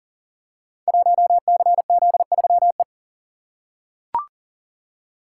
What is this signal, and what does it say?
Morse code, heard as a keyed tone, about 38 wpm, 700 Hz. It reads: 1CZ3E
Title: Code Groups - 5 Letter:Numbers - 40wpm